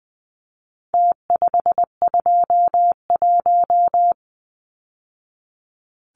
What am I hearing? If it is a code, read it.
Morse code, heard as a keyed tone, about 20 wpm, 700 Hz. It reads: T521
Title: Numbers - Contest Style - 4-Digit 20wpm